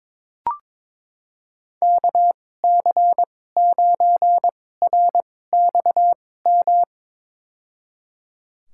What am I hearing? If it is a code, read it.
Morse code, heard as a keyed tone, about 22 wpm, spaced as for 16 wpm, 700 Hz. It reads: KC9RXM
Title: USA Call Signs - 6 Characters - 2x Character Spacing - 22wpm